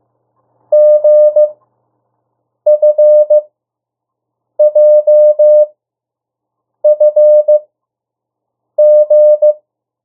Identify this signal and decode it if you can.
Morse code, heard as a keyed tone, about 15 wpm, spaced as for 6 wpm, 600 Hz. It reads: GFJFG
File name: urok 1